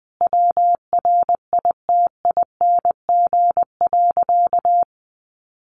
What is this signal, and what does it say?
Morse code, heard as a keyed tone, about 20 wpm, 700 Hz. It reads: WRITING.
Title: Sentences from Top 800 Words 20wpm